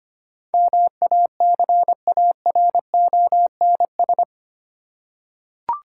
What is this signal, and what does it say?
Morse code, heard as a keyed tone, about 25 wpm, 700 Hz. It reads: MACARONS
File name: practice / 25wpm / Difficult Words with Hints - 25wpm